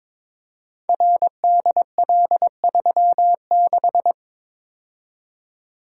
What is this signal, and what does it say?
Morse code, heard as a keyed tone, about 23 wpm, 700 Hz. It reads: RDL36
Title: Code Groups - 5 Letter:Numbers - 22wpm